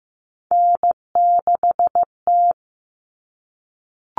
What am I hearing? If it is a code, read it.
Morse code, heard as a keyed tone, about 15 wpm, 700 Hz. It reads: N6T
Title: All Cut Numbers - 3-Digit 15wpm